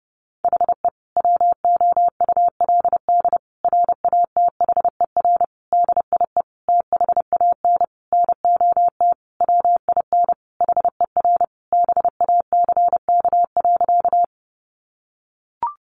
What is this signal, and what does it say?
Morse code, heard as a keyed tone, about 30 wpm, 700 Hz. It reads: HE WOULD RATHER DIE THAN NOT WIN HER BACK.
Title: Sentences from Top 1500 Words 30wpm